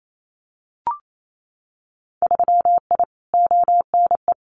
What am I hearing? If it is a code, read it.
Morse code, heard as a keyed tone, about 28 wpm, 700 Hz. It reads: 3I ONE
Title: Winter Field Day Exchanges Encore 28wpm